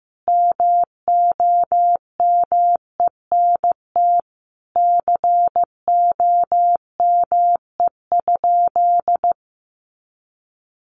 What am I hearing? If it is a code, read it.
Morse code, heard as a keyed tone, about 15 wpm, 700 Hz. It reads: MOMENT COME?
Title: Sentences from Top 1000 Words 15wpm